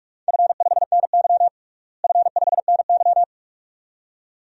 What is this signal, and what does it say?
Morse code, heard as a keyed tone, about 42 wpm, 700 Hz. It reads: F5NY F5NY
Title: DXCC-100 Call Signs - 4 Characters - Repeated 2x - 45wpm